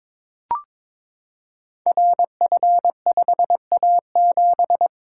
Morse code, heard as a keyed tone, about 23 wpm, 700 Hz. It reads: RF5A7
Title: Code Groups - 5 Letter:Numbers - 22wpm